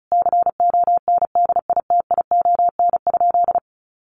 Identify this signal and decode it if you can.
Morse code, heard as a keyed tone, about 34 wpm, 700 Hz. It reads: CONDITION?
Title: Sentences from Top 1000 Words 35wpm